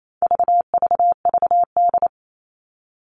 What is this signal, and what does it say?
Morse code, heard as a keyed tone, about 28 wpm, 700 Hz. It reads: VVVD
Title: All Cut Numbers - 4-Digit 28wpm